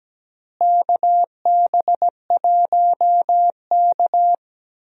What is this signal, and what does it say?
Morse code, heard as a keyed tone, about 17 wpm, 700 Hz. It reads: KB1K
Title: USA Call Signs - 4 Characters - Repeated 3x - 17wpm